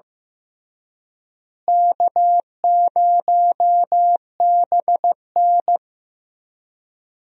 Morse code, heard as a keyed tone, about 16 wpm, 700 Hz. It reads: K0BN
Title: USA Call Signs - 4 Characters - Repeated 3x - 15wpm